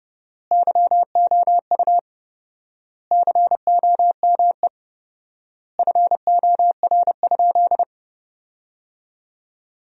Morse code, heard as a keyed tone, about 30 wpm, 700 Hz. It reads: YOU COME FOR?
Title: 4x Word-Spacing - Sentences from Top 100 Words - 30wpm